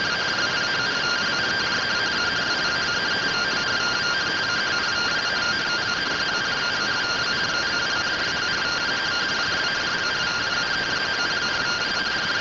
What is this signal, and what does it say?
Two-tone teleprinter signal, about 170 Hz shift.
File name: findings / MR AXIAL 33,99mm 1.gen-34 rtty rx